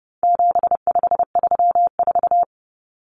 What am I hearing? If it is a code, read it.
Morse code, heard as a keyed tone, about 30 wpm, 700 Hz. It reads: 7534
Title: Common Cut Numbers - 4-Digit 30wpm